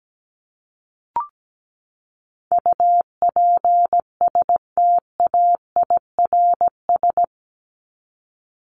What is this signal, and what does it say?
Morse code, heard as a keyed tone, about 17 wpm, 700 Hz. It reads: UPSTAIRS
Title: Top 1751-2000 Words 17wpm